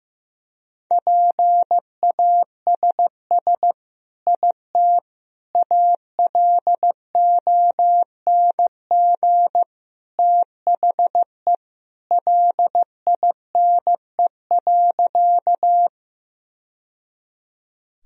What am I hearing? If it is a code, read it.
Morse code, heard as a keyed tone, about 16 wpm, 700 Hz. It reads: PASS IT ALONG THE LINE.
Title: Sentences from Top 1000 Words - Review Part 1 - 15wpm